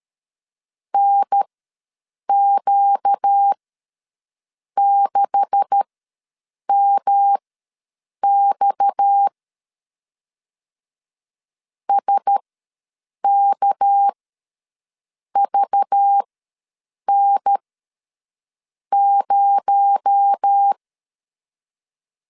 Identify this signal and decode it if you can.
Morse code, heard as a keyed tone, about 13 wpm, spaced as for 6 wpm, 800 Hz. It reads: NQ6MX SKVN0
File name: LEZIONI / 9 286 ws 12wpm 3351